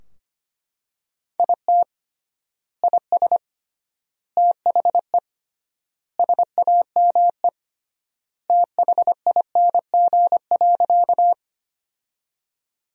Morse code, heard as a keyed tone, about 25 wpm, 700 Hz. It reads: IT IS THE SAME THING.
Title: 3x Word-Spacing - Sentences from Top 200 Words - 25wpm